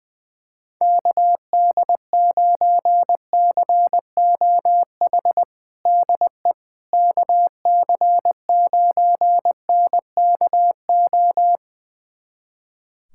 Morse code, heard as a keyed tone, about 20 wpm, 700 Hz. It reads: KD9COH DE KC9NKO
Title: USA Callsign DE Callsign - 6 Characters - 20wpm